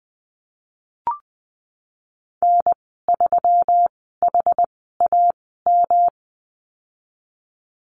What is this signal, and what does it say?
Morse code, heard as a keyed tone, about 20 wpm, spaced as for 15 wpm, 700 Hz. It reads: N3HAM
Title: 122 Lesson 20g - Number 3 - Call Signs 20wpm